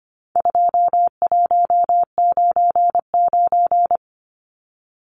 Morse code, heard as a keyed tone, about 25 wpm, 700 Hz. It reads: 2199
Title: Numbers - 4-Digit 25wpm